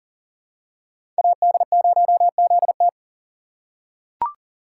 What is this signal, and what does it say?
Morse code, heard as a keyed tone, about 40 wpm, 700 Hz. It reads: AD0ZT
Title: USA Call Signs - 5 Characters - 3x Character Spacing - 40wpm